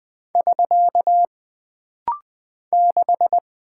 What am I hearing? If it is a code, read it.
Morse code, heard as a keyed tone, about 20 wpm, 700 Hz. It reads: <SK> 6
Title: Single Character (Letter:Number:Punctuation:Prosign) - Rapid-Fire - 20wpm